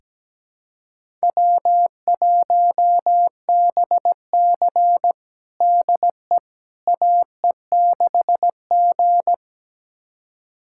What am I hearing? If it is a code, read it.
Morse code, heard as a keyed tone, about 17 wpm, 700 Hz. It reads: W1BC DE AE6G
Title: USA Callsign DE Callsign - 4 Characters - 17wpm